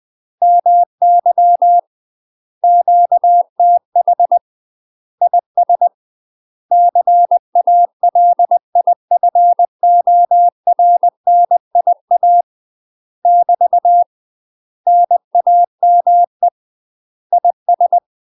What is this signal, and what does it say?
Morse code, heard as a keyed tone, about 20 wpm, 700 Hz. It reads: MY QTH IS CALIFORNIA <BT> NAME IS
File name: UK MASTER QSOs TRAINING / 20WPM / UK MASTER QSOs 0066